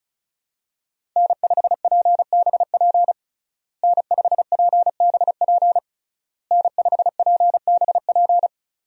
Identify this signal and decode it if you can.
Morse code, heard as a keyed tone, about 36 wpm, 700 Hz. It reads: N5PBP N5PBP N5PBP
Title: USA Call Signs - 5 Characters - Repeated 3x - 35wpm